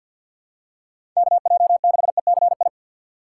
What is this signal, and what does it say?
Morse code, heard as a keyed tone, about 53 wpm, 700 Hz. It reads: KJ6ECI
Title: USA Call Signs - 6 Characters - 4x Character Spacing - 50wpm